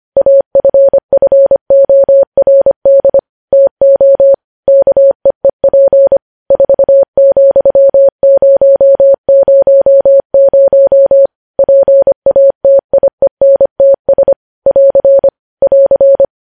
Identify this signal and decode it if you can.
Morse code, heard as a keyed tone, about 25 wpm, 550 Hz. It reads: AFFORD TO KEEP 4,000 PATIENTS <AR> <AR>
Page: news headlines in morse code at 25 wpm